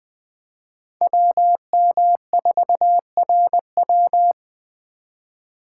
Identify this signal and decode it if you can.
Morse code, heard as a keyed tone, about 21 wpm, 700 Hz. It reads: WM4RW